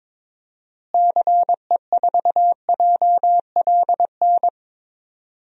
Morse code, heard as a keyed tone, about 22 wpm, 700 Hz. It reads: CE4JLN
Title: DXCC-100 Call Signs - 6 Characters - Repeated 3x - 22wpm